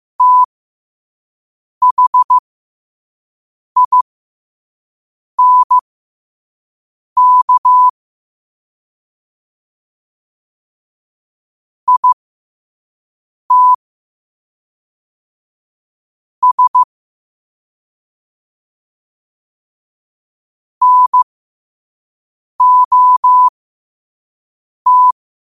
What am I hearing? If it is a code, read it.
Morse code, heard as a keyed tone, about 15 wpm, spaced as for 5 wpm, 1000 Hz. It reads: THINK IT S NOT